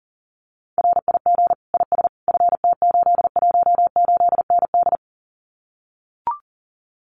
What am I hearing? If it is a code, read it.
Morse code, heard as a keyed tone, about 40 wpm, 700 Hz. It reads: RIG IS FT818ND
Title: Common QSO Phrases Encore 40wpm